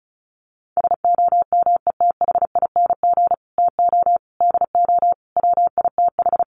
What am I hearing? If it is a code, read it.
Morse code, heard as a keyed tone, about 34 wpm, 700 Hz. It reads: SOMETHING TO DO WITH
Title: Sets of 5 Words Encore 35wpm